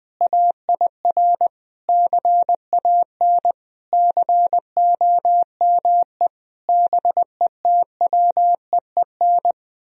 Morse code, heard as a keyed tone, about 20 wpm, 700 Hz. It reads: AIR CAN COME BETWEEN
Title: Sentences from Top 1000 Words - Review Part 1 - 20wpm